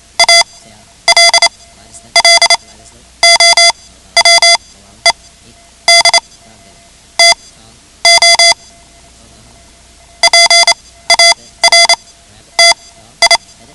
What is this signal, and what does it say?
Morse code, heard as a keyed tone, about 28 wpm, spaced as for 12 wpm, 2050 Hz. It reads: ALLOWED TO PARTI
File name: ham radio / CodeQuick / Code Quick 10